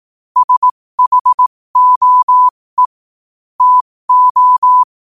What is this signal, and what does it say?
Morse code, heard as a keyed tone, about 18 wpm, spaced as for 15 wpm, 1000 Hz. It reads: SHOE TO
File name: qotdfiles / 15wpm